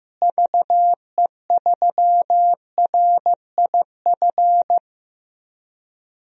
Morse code, heard as a keyed tone, about 16 wpm, 700 Hz. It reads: VE3RIF
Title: DXCC-100 Call Signs - 6 Characters - Repeated 3x - 15wpm